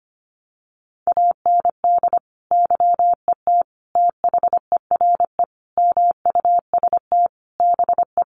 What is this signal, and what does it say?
Morse code, heard as a keyed tone, about 25 wpm, 700 Hz. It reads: AND YET THERE MUST BE
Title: Sentences from Top 1300 Words 25wpm